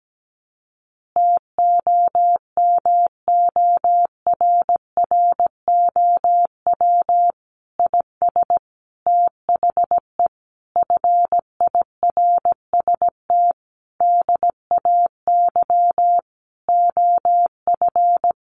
Morse code, heard as a keyed tone, about 17 wpm, 700 Hz. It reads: TOMORROW IS THE FIRST DAY OF
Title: Additional Easy Phrases 17wpm